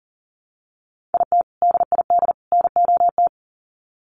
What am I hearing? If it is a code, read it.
Morse code, heard as a keyed tone, about 40 wpm, 700 Hz. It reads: IT DID NOT
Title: Sets of 3 Words 40wpm